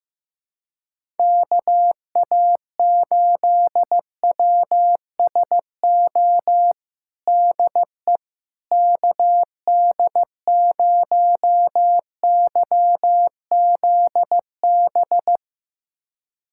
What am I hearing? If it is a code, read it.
Morse code, heard as a keyed tone, about 15 wpm, 700 Hz. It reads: KA8WSO DE KD0YZB
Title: USA Callsign DE Callsign - 6 Characters - 15wpm